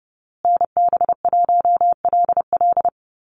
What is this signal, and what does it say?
Morse code, heard as a keyed tone, about 30 wpm, 700 Hz. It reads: NB1LL
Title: USA Call Signs Repeated 3x 30wpm